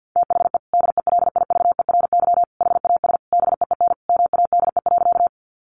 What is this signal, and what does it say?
Morse code, heard as a keyed tone, about 50 wpm, 700 Hz. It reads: THE DELIVERY HAS BEEN MADE.
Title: Sentences from Top 1500 Words 50wpm